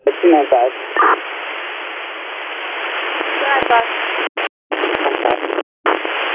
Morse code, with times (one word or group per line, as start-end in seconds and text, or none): none